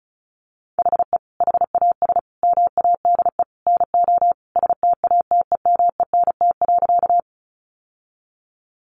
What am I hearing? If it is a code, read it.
Morse code, heard as a keyed tone, about 34 wpm, 700 Hz. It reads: HE HAS MADE NO STATEMENT.
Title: Sentences from Top 1200 Words 35wpm